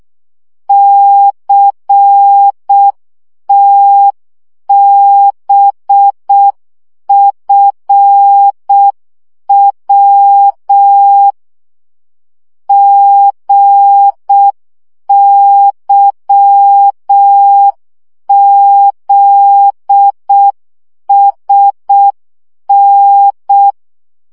Morse code, heard as a keyed tone, about 6 wpm, 800 Hz. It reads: CTBFW GYZSN
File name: morze vizsga 2011-2 5